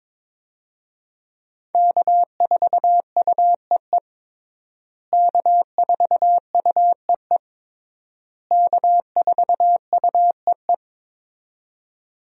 Morse code, heard as a keyed tone, about 21 wpm, 700 Hz. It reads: K4UEE K4UEE K4UEE
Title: USA Call Signs - 5 Characters - Repeated 3x - 22wpm